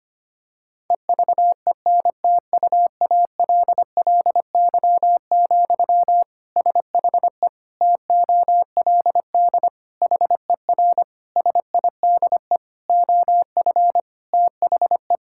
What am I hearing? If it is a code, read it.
Morse code, heard as a keyed tone, about 25 wpm, 700 Hz. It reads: EVENTUALLY, SHE TOLD HER SIDE OF THE
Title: Sentences from Top 1000 Words - Review Part 6 - 25wpm